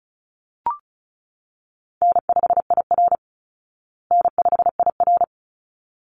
Morse code, heard as a keyed tone, about 34 wpm, 700 Hz. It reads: N5IR N5IR
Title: Call Signs Repeated 3x - 35wpm